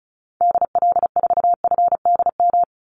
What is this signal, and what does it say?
Morse code, heard as a keyed tone, about 34 wpm, 700 Hz. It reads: DL4FDM
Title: Call Signs Repeated 2x - 35wpm